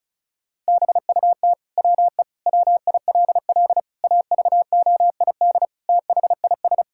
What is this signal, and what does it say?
Morse code, heard as a keyed tone, about 34 wpm, 700 Hz. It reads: BUT WE WILL AVOID THIS
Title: Sentences from Top 1000 Words - Review Part 6 - 35wpm